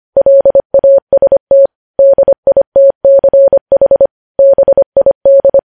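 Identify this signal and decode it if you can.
Morse code, heard as a keyed tone, about 25 wpm, 550 Hz. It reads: LAST DITCH BID
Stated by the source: Morse code news headlines sent at 25 wpm.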